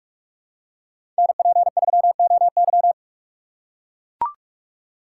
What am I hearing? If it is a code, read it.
Morse code, heard as a keyed tone, about 43 wpm, 700 Hz. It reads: NP3OY
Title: USA Call Signs - 5 Characters - 2x Character Spacing - 45wpm